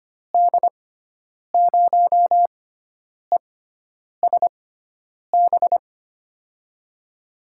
Morse code, heard as a keyed tone, about 25 wpm, spaced as for 9 wpm, 700 Hz. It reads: D0ESB
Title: USA Call Signs - 6 Characters - 6x Character Spacing - 25wpm